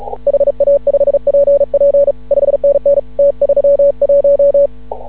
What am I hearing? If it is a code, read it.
Morse code, heard as a keyed tone, about 36 wpm, 560 Hz. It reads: HA5PP 5NN T31
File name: CW / ha6oi 2